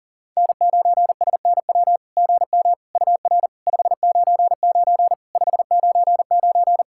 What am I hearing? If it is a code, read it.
Morse code, heard as a keyed tone, about 44 wpm, 700 Hz. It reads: N9SNW GM UR 599 599
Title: SOTA QSO Practice 40wpm